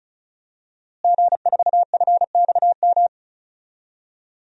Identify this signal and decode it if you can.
Morse code, heard as a keyed tone, about 35 wpm, 700 Hz. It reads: G4FXM
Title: DXCC-100 Call Signs - 5 Characters - Repeated 2x - 35wpm